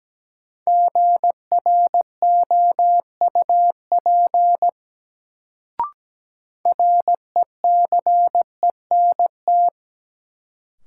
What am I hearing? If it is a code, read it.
Morse code, heard as a keyed tone, about 17 wpm, 700 Hz. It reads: GROUP RECENT
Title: Sets of 3 Words - New Word Drill - 17wpm